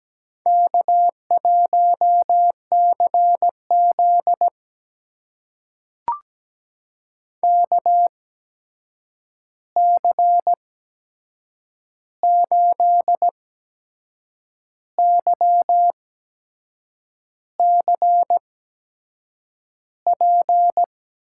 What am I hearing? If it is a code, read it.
Morse code, heard as a keyed tone, about 17 wpm, spaced as for 6 wpm, 700 Hz. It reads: K1CZ KC8YCP
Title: USA Call Signs - 8x Character Spacing - 17wpm